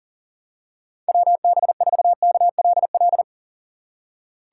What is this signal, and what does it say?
Morse code, heard as a keyed tone, about 40 wpm, 700 Hz. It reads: WB4KLL
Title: USA Call Signs - 6x Character Spacing - 40wpm